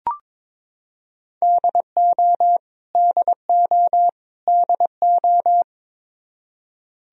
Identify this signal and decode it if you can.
Morse code, heard as a keyed tone, about 22 wpm, 700 Hz. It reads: DO DO DO
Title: Top 100 Words Repeated 22wpm